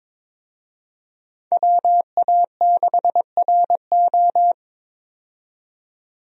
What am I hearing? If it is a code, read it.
Morse code, heard as a keyed tone, about 22 wpm, 700 Hz. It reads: WA6RO
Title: USA Call Signs - 5 Characters - 22wpm